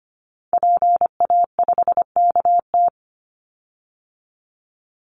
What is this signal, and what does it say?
Morse code, heard as a keyed tone, about 25 wpm, 700 Hz. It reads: PA5KT
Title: Call Signs Repeated 3x - 25wpm